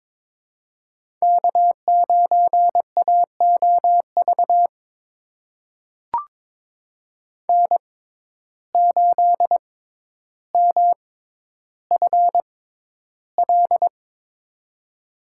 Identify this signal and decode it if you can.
Morse code, heard as a keyed tone, about 22 wpm, spaced as for 11 wpm, 700 Hz. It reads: K9AOV N8MFL
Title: USA Call Signs - 5 Characters - 6x Character Spacing - 22wpm